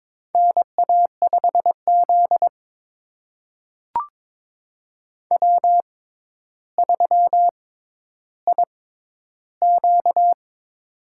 Morse code, heard as a keyed tone, about 21 wpm, spaced as for 11 wpm, 700 Hz. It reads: NA5Z W3IQ
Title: USA Call Signs - 4 Characters - 6x Character Spacing - 22wpm